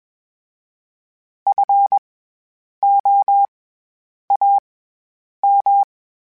Morse code, heard as a keyed tone, about 21 wpm, spaced as for 8 wpm, 800 Hz. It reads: FOAM
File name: Special INGs-mp3s / Special INGs-mp3s / SlowToFaster ing @20